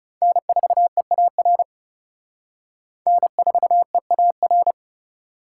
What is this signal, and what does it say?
Morse code, heard as a keyed tone, about 35 wpm, 700 Hz. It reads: N4EAR N4EAR
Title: Speed Racing - USA 1x3 Call Signs 30wpm